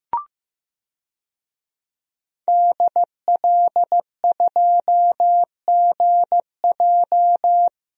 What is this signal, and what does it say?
Morse code, heard as a keyed tone, about 15 wpm, 700 Hz. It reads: DL2GJ
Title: DXCC-100 Call Signs - 5 Characters - Repeated 2x - 15wpm